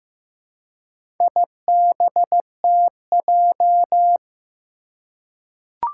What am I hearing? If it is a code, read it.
Morse code, heard as a keyed tone, about 15 wpm, 700 Hz. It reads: IBTJ
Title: Four Challenging Letter-Number Combinations 15wpm